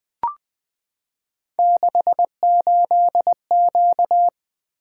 Morse code, heard as a keyed tone, about 20 wpm, 700 Hz. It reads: 68Q
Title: Three Letter-Number Combinations - 20wpm